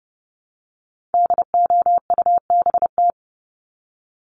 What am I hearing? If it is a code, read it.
Morse code, heard as a keyed tone, about 30 wpm, 700 Hz. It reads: DOUBT
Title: Top 1251-1500 Words 30wpm